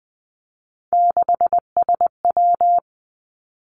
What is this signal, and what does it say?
Morse code, heard as a keyed tone, about 20 wpm, 700 Hz. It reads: 6SW
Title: Three Challenging Letter-Number Combinations 20wpm